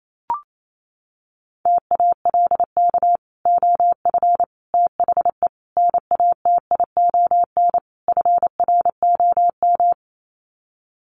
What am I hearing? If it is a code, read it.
Morse code, heard as a keyed tone, about 28 wpm, 700 Hz. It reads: TALK OF THE NATION FROM
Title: Sets of 5 Words 28wpm